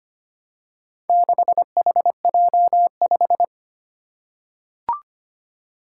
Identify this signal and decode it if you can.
Morse code, heard as a keyed tone, about 25 wpm, 700 Hz. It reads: 6HJ5
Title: Four Challenging Letter-Number Combinations 25wpm